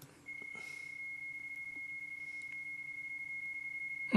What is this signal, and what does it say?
PSK31, about 31.25 baud.